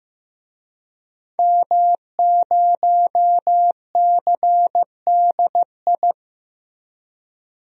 Morse code, heard as a keyed tone, about 15 wpm, 700 Hz. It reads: M0CDI
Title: DXCC-100 Call Signs - 5 Characters - Repeated 3x - 15wpm